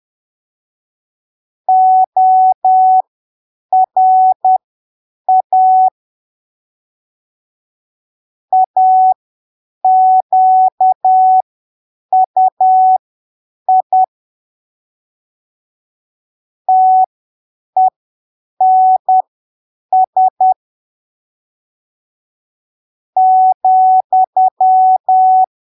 Morse code, heard as a keyed tone, about 10 wpm, spaced as for 7 wpm, 750 Hz. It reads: ORA AQUI TENS ,
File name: arquivos / texto / texto 26 10ppm